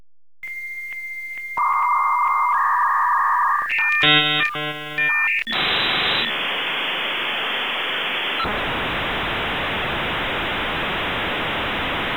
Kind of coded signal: dial-up modem handshake